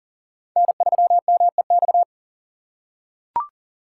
Morse code, heard as a keyed tone, about 38 wpm, 700 Hz. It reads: N3MEX
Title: USA Call Signs - 5 Characters - 3x Character Spacing - 40wpm